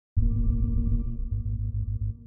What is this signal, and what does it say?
There is no code.